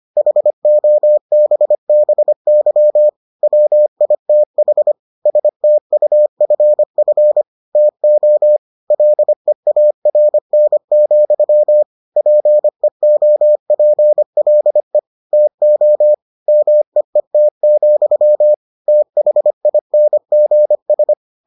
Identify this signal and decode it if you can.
Morse code, heard as a keyed tone, about 25 wpm, 600 Hz. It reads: HOBBY WITH STUFF TO LEARN, PEOPLE TO MEET, THINGS